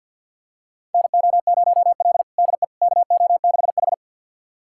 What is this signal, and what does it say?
Morse code, heard as a keyed tone, about 48 wpm, 690 Hz. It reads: NO0L DE KO6H
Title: USA Callsign DE Callsign - 4 Characters - 50wpm